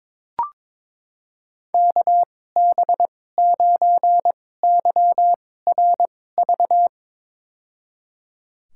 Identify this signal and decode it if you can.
Morse code, heard as a keyed tone, about 22 wpm, spaced as for 16 wpm, 700 Hz. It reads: KB9YRV
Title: USA Call Signs - 6 Characters - 2x Character Spacing - 22wpm